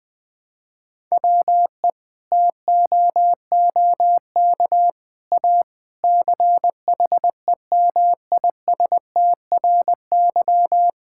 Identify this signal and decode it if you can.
Morse code, heard as a keyed tone, about 20 wpm, 700 Hz. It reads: WE TOOK A CHEMISTRY
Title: Sentences from Top 1000 Words - Review Part 6 - 20wpm